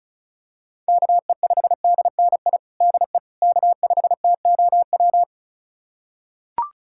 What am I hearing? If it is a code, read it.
Morse code, heard as a keyed tone, about 34 wpm, 700 Hz. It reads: KE5DNI DE K5TOW
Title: USA Callsign DE Callsign 35wpm